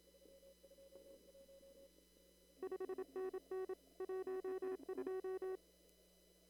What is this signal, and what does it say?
Morse code, heard as a keyed tone, about 27 wpm, 380 Hz. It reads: 5NN 12
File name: CW / kt5j wpx cw-op-k5pi